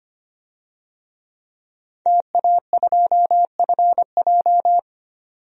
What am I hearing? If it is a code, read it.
Morse code, heard as a keyed tone, about 25 wpm, 690 Hz. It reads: TA2FJ
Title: DXCC-100 Call Signs - 5 Characters - 25wpm